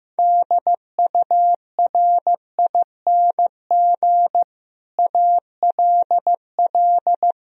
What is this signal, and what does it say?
Morse code, heard as a keyed tone, about 15 wpm, 700 Hz. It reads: DURING ALL
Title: Sentences from Top 1000 Words - Review Part 2 - 15wpm